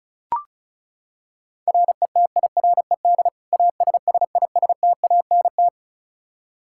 Morse code, heard as a keyed tone, about 36 wpm, 700 Hz. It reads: RETIRED ASSISTANT
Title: K1USN SST QSO Practice 35wpm